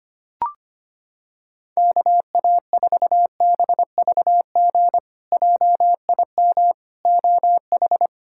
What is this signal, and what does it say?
Morse code, heard as a keyed tone, about 25 wpm, 700 Hz. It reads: KA4BVG JIM OH
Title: K1USN SST QSO Practice - 25wpm